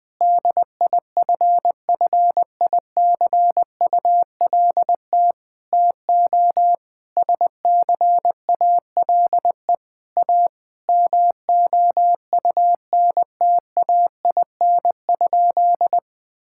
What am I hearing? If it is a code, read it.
Morse code, heard as a keyed tone, about 20 wpm, 700 Hz. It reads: DIFFICULT TO SCALE A MOUNTAIN?
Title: Sentences from Top 1000 Words - Review Part 5 - 20wpm